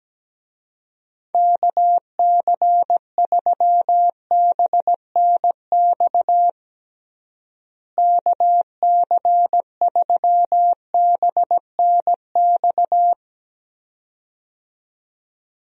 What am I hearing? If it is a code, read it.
Morse code, heard as a keyed tone, about 17 wpm, 700 Hz. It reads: KC3BNX KC3BNX
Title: USA Call Signs - 6 Characters - Repeated 2x - 17wpm